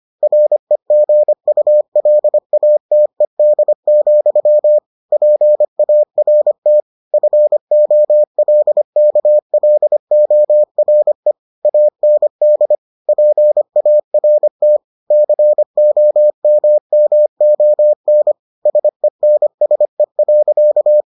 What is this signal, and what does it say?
Morse code, heard as a keyed tone, about 25 wpm, 600 Hz. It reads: REGULATED, PART FOLKLORE AND PART COMMON SENSE.